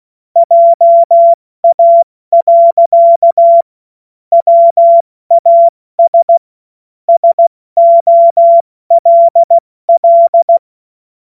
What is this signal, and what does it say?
Morse code, heard as a keyed tone, about 16 wpm, 680 Hz. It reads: JA. WAS SOLL
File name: Morsekurs / Witze / Witz 11. Januar 2022, 70 F80